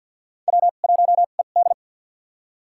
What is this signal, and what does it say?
Morse code, heard as a keyed tone, about 51 wpm, spaced as for 36 wpm, 700 Hz. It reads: W1ED
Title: USA Call Signs - 4 Characters - 2x Character Spacing - 50wpm